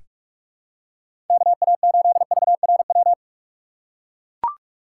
Morse code, heard as a keyed tone, about 43 wpm, 700 Hz. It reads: KA8VRW
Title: USA Call Signs - 6 Characters - 4x Character Spacing - 45wpm